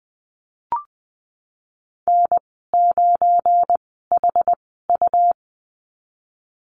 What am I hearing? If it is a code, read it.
Morse code, heard as a keyed tone, about 20 wpm, spaced as for 15 wpm, 700 Hz. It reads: N9HU
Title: 186 Lesson 30g - Letter b - Call Signs 20wpm